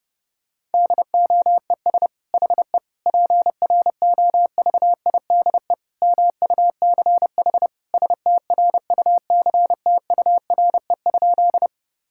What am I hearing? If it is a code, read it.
Morse code, heard as a keyed tone, about 32 wpm, 700 Hz. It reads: DOES HE PROVIDE MUCH STRUCTURE?
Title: Sentences from Top 1000 Words - Review Part 6 - 30wpm